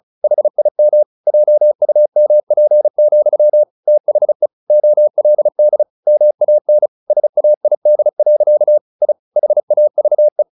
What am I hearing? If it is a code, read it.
Morse code, heard as a keyed tone, about 34 wpm, 600 Hz. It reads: HIM JUMP, THE OLD MAN SAID. I HAVE
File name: the old man and the sea / the old man and the sea 1 35wpm